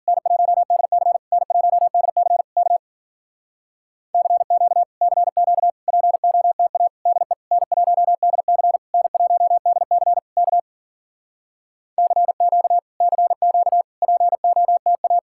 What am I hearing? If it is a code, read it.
Morse code, heard as a keyed tone, about 52 wpm, 700 Hz. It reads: N1DC N1DC K CQ CQ POTA DE N1DC N1DC K CQ CQ POTA